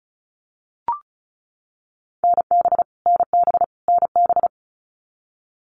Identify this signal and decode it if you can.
Morse code, heard as a keyed tone, about 34 wpm, 700 Hz. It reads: NB NB NB
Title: Canadian Provinces and Territories - Repeated 3x - 35wpm